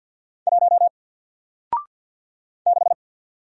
Morse code, heard as a keyed tone, about 50 wpm, 700 Hz. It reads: 1 6
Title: Single Letter-Number 50wpm